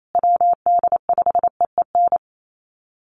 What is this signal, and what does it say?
Morse code, heard as a keyed tone, about 28 wpm, 700 Hz. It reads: WD5EEN
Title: USA Call Signs Repeated 3x 28wpm